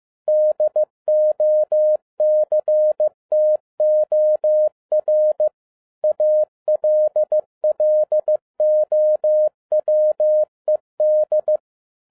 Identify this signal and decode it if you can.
Morse code, heard as a keyed tone, about 15 wpm, 610 Hz. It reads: DOCTOR ALLOWED
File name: 2011-07-mp3 / n57 BACK FROM A BIG ACCIDENT 15wpm